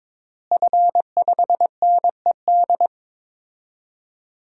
Morse code, heard as a keyed tone, about 22 wpm, 700 Hz. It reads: F5NED
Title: DXCC-100 Call Signs - 5 Characters - Repeated 3x - 22wpm